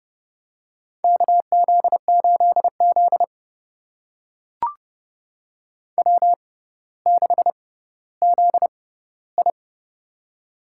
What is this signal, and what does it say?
Morse code, heard as a keyed tone, about 30 wpm, spaced as for 15 wpm, 700 Hz. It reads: KZ8Z W6ZI
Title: USA Call Signs - 4 Characters - 6x Character Spacing - 30wpm